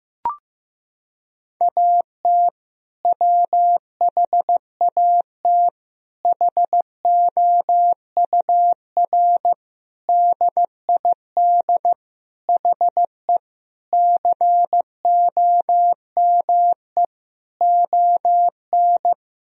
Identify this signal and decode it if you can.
Morse code, heard as a keyed tone, about 15 wpm, 700 Hz. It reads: AT WHAT HOUR DID HE COME ON
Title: Sentences from Top 1000 Words - Review Part 4 - 15wpm